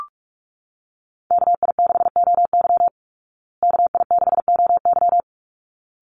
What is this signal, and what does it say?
Morse code, heard as a keyed tone, about 44 wpm, 700 Hz. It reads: KI6OY KI6OY
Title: Call Signs Repeated 3x - 45wpm